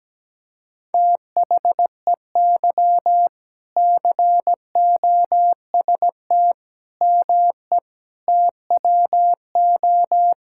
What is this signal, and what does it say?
Morse code, heard as a keyed tone, about 17 wpm, 700 Hz. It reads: THEY COST ME TWO
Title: Sentences from Top 1000 Words - Review Part 5 - 17wpm